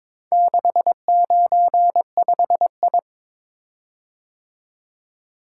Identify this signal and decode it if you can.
Morse code, heard as a keyed tone, about 22 wpm, 700 Hz. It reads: 695I
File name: practice / 22wpm / Four Letter-Number Combinations - 22wpm